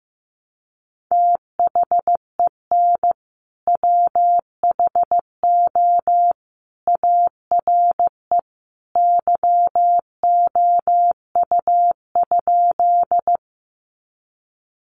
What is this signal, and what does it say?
Morse code, heard as a keyed tone, about 15 wpm, 700 Hz. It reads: THEN WHO ARE YOU?
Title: Sentences from Top 100 Words 15wpm